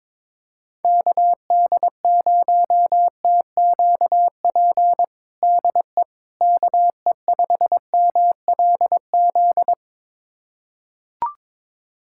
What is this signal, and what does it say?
Morse code, heard as a keyed tone, about 21 wpm, 700 Hz. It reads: KD0TQP DE KE5MLZ
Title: USA Callsign DE Callsign - 6 Characters - 22wpm